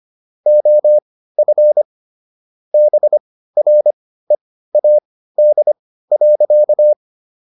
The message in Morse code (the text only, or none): OF BREAD.